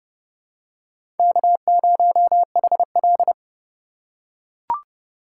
Morse code, heard as a keyed tone, about 30 wpm, 700 Hz. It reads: K0HL
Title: USA Call Signs - 4 Characters - 2x Character Spacing - 30wpm